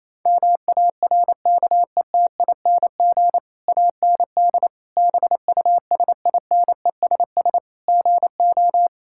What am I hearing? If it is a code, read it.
Morse code, heard as a keyed tone, about 28 wpm, 700 Hz. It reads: MARKETING AND BUSINESS GO
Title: Sentences from Top 1000 Words - Review Part 5 - 28wpm